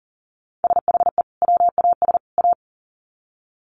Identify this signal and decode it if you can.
Morse code, heard as a keyed tone, about 40 wpm, 700 Hz. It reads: SHE WAS A
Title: Sets of 3 Words 40wpm